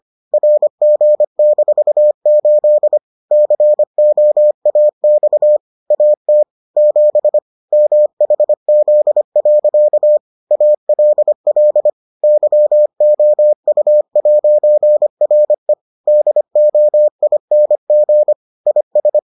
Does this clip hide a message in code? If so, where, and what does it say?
Yes: Morse code from 0.3 to 19.2 s, heard as a keyed tone, about 25 wpm, 600 Hz. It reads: RG-8 COAX AT 7 MHZ. ALL YOU'RE DOING IS